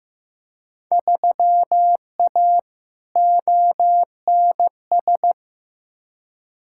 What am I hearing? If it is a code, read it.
Morse code, heard as a keyed tone, about 15 wpm, 700 Hz. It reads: 3A ONS
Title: Field Day Exchanges - 15wpm